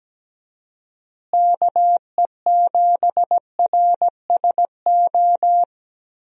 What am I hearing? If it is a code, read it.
Morse code, heard as a keyed tone, about 17 wpm, 700 Hz. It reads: KE7RSO
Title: USA Call Signs - 6 Characters - Repeated 3x - 17wpm